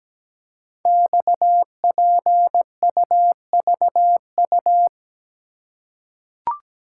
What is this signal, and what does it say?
Morse code, heard as a keyed tone, about 17 wpm, 700 Hz. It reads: XPUVU
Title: Code Groups - 5 Letters - 17wpm